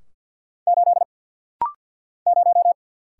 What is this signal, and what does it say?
Morse code, heard as a keyed tone, about 50 wpm, 700 Hz. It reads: 8 0
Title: Single Letter-Number - Rapid-Fire 50wpm